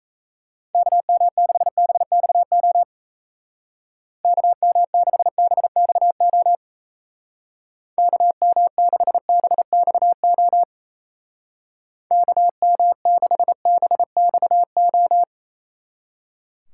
Morse code, heard as a keyed tone, about 34 wpm, 700 Hz. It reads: KM6BXO KM6BXO KM6BXO KM6BXO
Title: Speed Racing - USA 2x3 Call Signs 28wpm